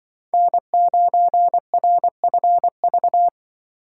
Morse code, heard as a keyed tone, about 23 wpm, 700 Hz. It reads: N9RFV
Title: Speed Racing - USA 1x3 Call Signs 20wpm